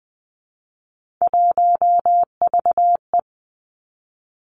Morse code, heard as a keyed tone, about 19 wpm, 700 Hz. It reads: 1VE
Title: Three Challenging Letter-Number Combinations 20wpm